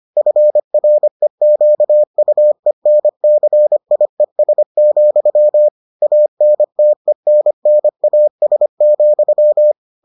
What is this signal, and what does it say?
Morse code, heard as a keyed tone, about 25 wpm, 600 Hz. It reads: FREQUENCIES, ANTENNAS,